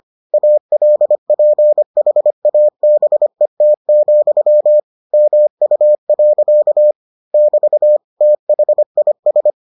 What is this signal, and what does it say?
Morse code, heard as a keyed tone, about 25 wpm, 600 Hz. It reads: ALPHABET, MU. <BT> THIS